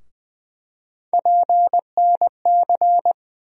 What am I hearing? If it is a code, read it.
Morse code, heard as a keyed tone, about 20 wpm, 700 Hz. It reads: PNC
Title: USA Call Sign Postfixes - 3 Letters 20wpm